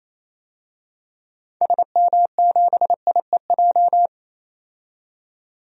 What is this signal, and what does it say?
Morse code, heard as a keyed tone, about 28 wpm, 700 Hz. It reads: SM7IEJ
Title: DXCC-100 Call Signs - 6 Characters - 28wpm